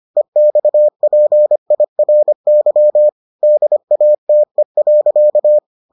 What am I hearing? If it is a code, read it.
Morse code, heard as a keyed tone, about 25 wpm, 600 Hz. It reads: EXPIRY DATE.